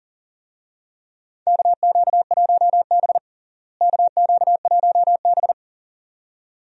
Morse code, heard as a keyed tone, about 40 wpm, 700 Hz. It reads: KQ1B KQ1B
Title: USA Call Signs - 4 Characters - Repeated 2x - 40wpm